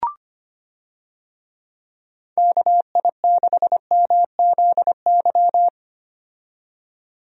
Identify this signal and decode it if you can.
Morse code, heard as a keyed tone, about 25 wpm, 700 Hz. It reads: KI6MZY
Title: USA Call Signs - 6 Characters - 25wpm